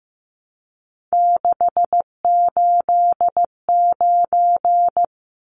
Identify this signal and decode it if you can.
Morse code, heard as a keyed tone, about 15 wpm, 700 Hz. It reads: 689
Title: Numbers - 3-Digit 15wpm